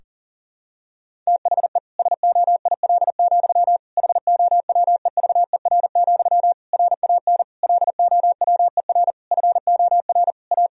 Morse code, heard as a keyed tone, about 40 wpm, 700 Hz. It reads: THE SOIL, HOWEVER, RAN LOWER FOR A
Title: Sentences from Top 1000 Words - Review Part 4 - 40wpm